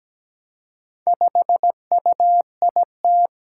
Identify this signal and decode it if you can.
Morse code, heard as a keyed tone, about 17 wpm, 700 Hz. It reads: 5UIT
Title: Four Challenging Letter-Number Combinations 17wpm